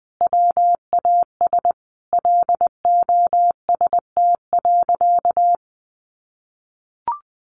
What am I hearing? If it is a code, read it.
Morse code, heard as a keyed tone, about 20 wpm, 700 Hz. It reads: WAS LOST.
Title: Sentences from Top 1400 Words 20wpm